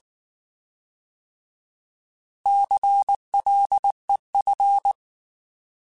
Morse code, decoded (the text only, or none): CLEF